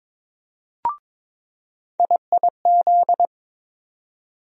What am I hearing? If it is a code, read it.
Morse code, heard as a keyed tone, about 22 wpm, 700 Hz. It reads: IIZ
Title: USA Call Sign Postfixes - 3 Letters 22wpm